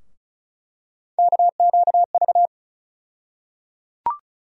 Morse code, heard as a keyed tone, about 36 wpm, 700 Hz. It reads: KQV
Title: USA Call Sign Postfixes - 3 Letters 35wpm